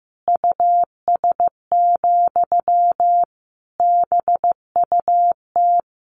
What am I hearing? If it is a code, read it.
Morse code, heard as a keyed tone, about 15 wpm, 700 Hz. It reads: US, BUT
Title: Sentences from Top 1500 Words 15wpm